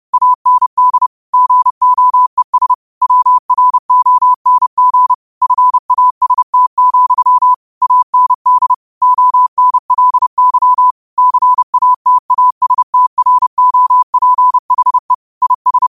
Morse code, heard as a keyed tone, about 30 wpm, 1000 Hz. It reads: AND GOES WRONG FAST, AND ONLY CATASTROPHE IS
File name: qotdfiles / 30wpm